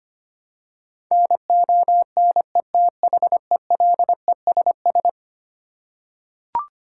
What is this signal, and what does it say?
Morse code, heard as a keyed tone, about 25 wpm, 700 Hz. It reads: NONETHELESS
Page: morse-code-ninja-practice-25wpm